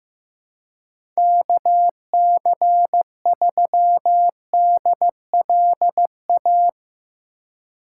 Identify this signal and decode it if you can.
Morse code, heard as a keyed tone, about 16 wpm, 700 Hz. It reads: KC3DLA
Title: USA Call Signs - 6 Characters - 4x Character Spacing - 15wpm